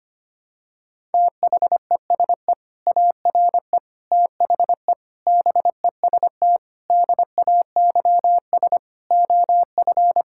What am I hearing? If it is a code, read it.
Morse code, heard as a keyed tone, about 25 wpm, 700 Hz. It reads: THESE ARE THE BEST DAYS OF
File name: practice / 25wpm / Sentences with 2-Word Phrases - 25wpm